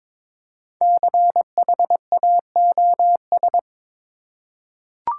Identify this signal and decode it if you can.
Morse code, heard as a keyed tone, about 22 wpm, 700 Hz. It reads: CHAOS
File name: 5 Letter Words - First 3 Letters Given - 22wpm